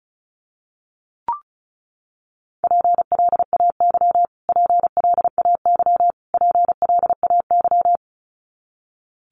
Morse code, heard as a keyed tone, about 34 wpm, 700 Hz. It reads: PLAY PLAY PLAY
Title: Top 500 Words Repeated 35wpm